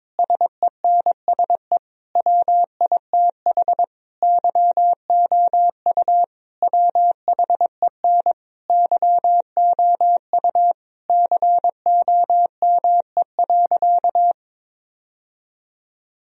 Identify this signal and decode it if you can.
Morse code, heard as a keyed tone, about 21 wpm, 700 Hz. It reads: SENSE WITH YOU WHEN YOU COME.
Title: Sentences from Top 1000 Words - Review Part 4 - 22wpm